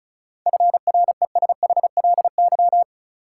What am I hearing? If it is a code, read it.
Morse code, heard as a keyed tone, about 36 wpm, 700 Hz. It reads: FRESHLY
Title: Sets of 5 Words Encore - New Word Drill - 35wpm